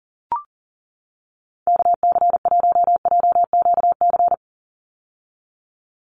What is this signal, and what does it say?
Morse code, heard as a keyed tone, about 40 wpm, 700 Hz. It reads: KC1JQC
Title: USA Call Signs 40wpm